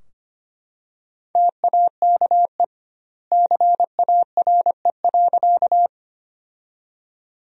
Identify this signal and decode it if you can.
Morse code, heard as a keyed tone, about 25 wpm, 700 Hz. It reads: TAKE CARE.
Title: 2x Word-Spacing - Sentences from Top 300 Words - 25wpm